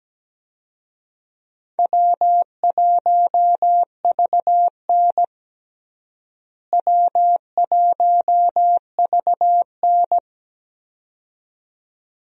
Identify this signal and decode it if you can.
Morse code, heard as a keyed tone, about 17 wpm, 700 Hz. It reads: W1VN W1VN
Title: USA Call Signs - 4 Characters - Repeated 2x - 17wpm